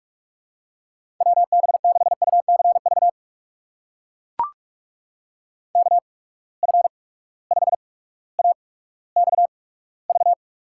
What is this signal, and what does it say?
Morse code, heard as a keyed tone, about 44 wpm, spaced as for 19 wpm, 700 Hz. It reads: WB6UCV KF5AXV
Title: USA Call Signs - 6 Characters - 8x Character Spacing - 45wpm